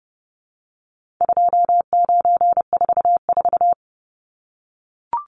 Morse code, heard as a keyed tone, about 30 wpm, 700 Hz. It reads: 2944
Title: Numbers - 4-Digit 30wpm